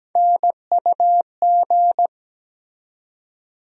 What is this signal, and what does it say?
Morse code, heard as a keyed tone, about 17 wpm, 700 Hz. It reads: NUG
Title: USA Call Sign Postfixes - 3 Letters 17wpm